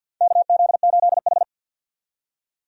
Morse code, heard as a keyed tone, about 53 wpm, 700 Hz. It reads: KZ8H
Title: USA Call Signs - 4 Characters - 50wpm